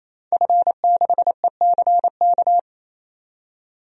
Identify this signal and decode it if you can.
Morse code, heard as a keyed tone, about 27 wpm, 700 Hz. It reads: F6ECK
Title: DXCC-100 Call Signs - 5 Characters - 28wpm